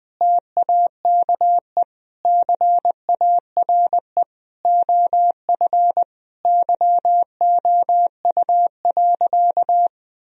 Morse code, heard as a keyed tone, about 20 wpm, 700 Hz. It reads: TAKE CARE OF YOU.
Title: Sentences from Top 1000 Words - Review Part 1 - 20wpm